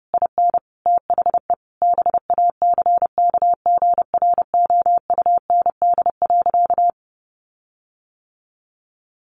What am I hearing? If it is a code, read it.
Morse code, heard as a keyed tone, about 30 wpm, 700 Hz. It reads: IN THE BACKGROUND.
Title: Sentences from Top 1300 Words 30wpm